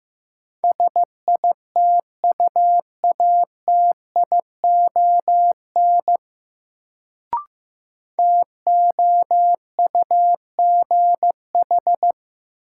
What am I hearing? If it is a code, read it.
Morse code, heard as a keyed tone, about 15 wpm, 700 Hz. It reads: SITUATION TOUGH
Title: Sentences from Top 1200 Words - New Word Drill - 15wpm